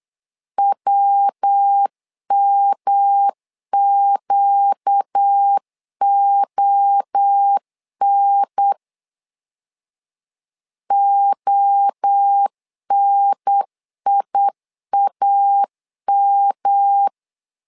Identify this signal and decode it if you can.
Morse code, heard as a keyed tone, about 8 wpm, 800 Hz. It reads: WMQON ONIAM
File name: LEZIONI / 4 Q1 8wpm 3035